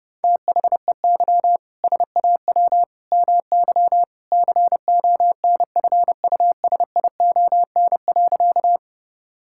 Morse code, heard as a keyed tone, about 30 wpm, 700 Hz. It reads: THEY SAW MY CONFUSION.
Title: Sentences from Top 1600 Words 30wpm